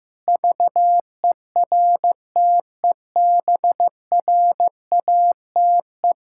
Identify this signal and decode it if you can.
Morse code, heard as a keyed tone, about 16 wpm, 700 Hz. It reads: VERTEBRATE
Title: Pangram Challenge v2 - 15wpm